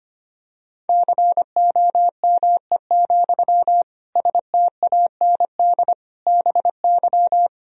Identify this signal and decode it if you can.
Morse code, heard as a keyed tone, about 25 wpm, 700 Hz. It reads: COME, STAND BY
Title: Sentences from Top 1000 Words - Review Part 1 - 25wpm